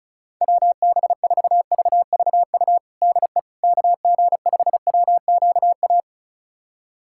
Morse code, heard as a keyed tone, about 36 wpm, 700 Hz. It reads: WB4VVU DE KG5WQA
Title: USA Callsign DE Callsign - 6 Characters - 35wpm